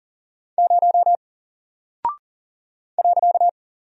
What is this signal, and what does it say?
Morse code, heard as a keyed tone, about 40 wpm, 700 Hz. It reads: 0 .
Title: Single Character (Letter:Number:Punctuation:Prosign) Alt 40wpm